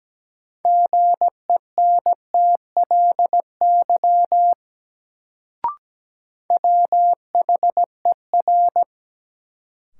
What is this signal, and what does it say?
Morse code, heard as a keyed tone, about 17 wpm, 700 Hz. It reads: GENTLY WHER
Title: Sentences from Top 1500 Words - New Word Drill Alt - 17wpm